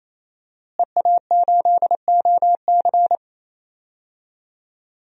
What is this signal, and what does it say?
Morse code, heard as a keyed tone, about 28 wpm, 700 Hz. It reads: EA8OC
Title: DXCC-100 Call Signs - 5 Characters - Repeated 3x - 28wpm